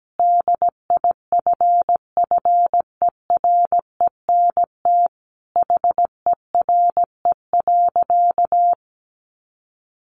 Morse code, heard as a keyed tone, about 17 wpm, 700 Hz. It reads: DIFFERENT HERE.
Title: Easy Phrases 17wpm